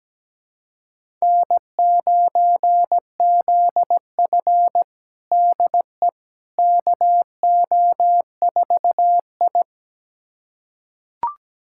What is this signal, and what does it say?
Morse code, heard as a keyed tone, about 17 wpm, 700 Hz. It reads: N9ZF DE KO4I
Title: USA Callsign DE Callsign - 4 Characters - 17wpm